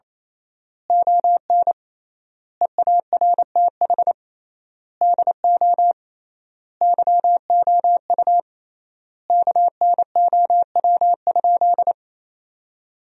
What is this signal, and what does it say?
Morse code, heard as a keyed tone, about 28 wpm, 700 Hz. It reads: ON EARTH DO YOU KNOW?
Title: 3x Word-Spacing - Sentences from Top 200 Words - 28wpm